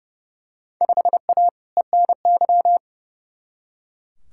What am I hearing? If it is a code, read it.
Morse code, heard as a keyed tone, about 30 wpm, 700 Hz. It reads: 5A ENY
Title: Field Day Exchanges Encore - 30wpm